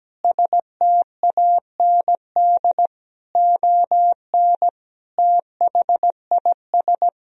Morse code, heard as a keyed tone, about 17 wpm, 700 Hz. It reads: STAND ON THIS
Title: Sentences from Top 1000 Words - Review Part 2 - 17wpm